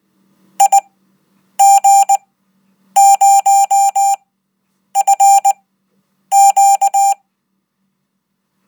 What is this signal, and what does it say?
Morse code, heard as a keyed tone, about 20 wpm, spaced as for 9 wpm, 790 Hz. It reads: IG0FQ